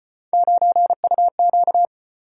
Morse code, heard as a keyed tone, about 34 wpm, 700 Hz. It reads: 9UQ